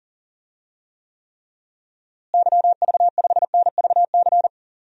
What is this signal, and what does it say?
Morse code, heard as a keyed tone, about 40 wpm, 700 Hz. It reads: YV5NVC
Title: DXCC-100 Call Signs - 6 Characters - Repeated 3x - 40wpm